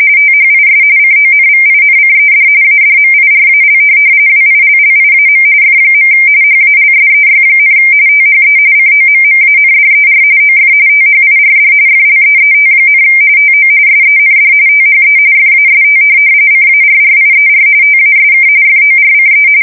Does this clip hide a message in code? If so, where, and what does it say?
Yes: RTTY from 0.2 to 19.5 s, about 45.45 baud. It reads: THIS CONVINCED ADMIRAL WILLIAM F. HALSEY, THE COMMANDER OF THIRD FLEET, TO SEND THE FAST CARRIER TASK FORCE